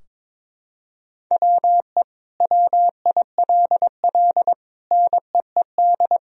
Morse code, heard as a keyed tone, about 22 wpm, 700 Hz. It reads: WE WILL NEED
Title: Sentences from Top 1000 Words - Review Part 4 - 22wpm